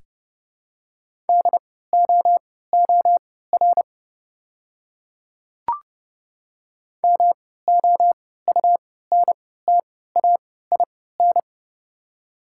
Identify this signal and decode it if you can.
Morse code, heard as a keyed tone, about 30 wpm, spaced as for 17 wpm, 700 Hz. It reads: DOOR MOUNTAIN
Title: 3x Character Spacing - Top 300 Words - 30wpm